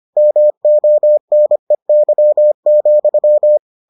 Morse code, heard as a keyed tone, about 25 wpm, 600 Hz. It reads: MONEY,